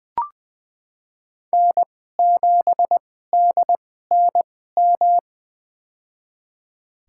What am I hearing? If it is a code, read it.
Morse code, heard as a keyed tone, about 20 wpm, spaced as for 14 wpm, 700 Hz. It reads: N7DNM
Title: USA Call Signs - 5 Characters - 2x Character Spacing - 20wpm